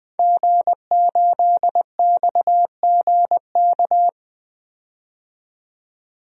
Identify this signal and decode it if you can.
Morse code, heard as a keyed tone, about 20 wpm, 700 Hz. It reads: G8XGK
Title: DXCC-100 Call Signs - 5 Characters - 20wpm